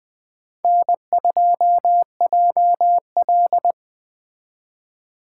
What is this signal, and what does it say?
Morse code, heard as a keyed tone, about 20 wpm, 700 Hz. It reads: N2JL
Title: Speed Racing - USA 1x2 Call Signs 17wpm